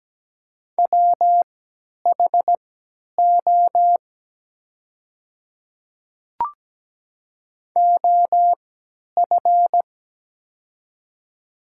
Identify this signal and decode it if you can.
Morse code, heard as a keyed tone, about 17 wpm, spaced as for 10 wpm, 700 Hz. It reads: WHO OF
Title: 3x Character Spacing - Top 100 Words - 17wpm